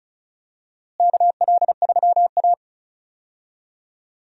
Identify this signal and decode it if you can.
Morse code, heard as a keyed tone, about 36 wpm, 700 Hz. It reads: KL3A
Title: USA Call Signs - 4 Characters - Repeated 2x - 35wpm